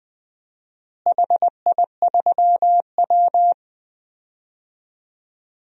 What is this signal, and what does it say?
Morse code, heard as a keyed tone, about 20 wpm, 700 Hz. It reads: HI3W
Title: DXCC-100 Call Signs - 4 Characters - Repeated 2x - 20wpm